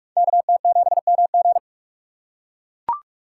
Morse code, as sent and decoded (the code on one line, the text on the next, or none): -.- - --... -- --.
KT7MG